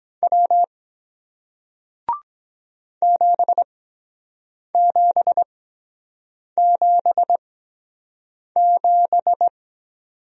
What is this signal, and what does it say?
Morse code, heard as a keyed tone, about 23 wpm, spaced as for 7 wpm, 700 Hz. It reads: W 7777